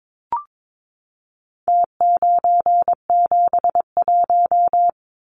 Morse code, heard as a keyed tone, about 22 wpm, 700 Hz. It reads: T971
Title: Numbers - Contest Style - 4-Digit 22wpm